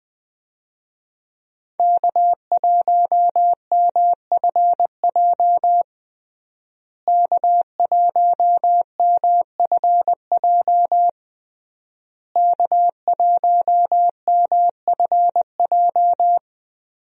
Morse code, heard as a keyed tone, about 20 wpm, 700 Hz. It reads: K1MFJ K1MFJ K1MFJ
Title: USA Call Signs - 5 Characters - Repeated 3x - 20wpm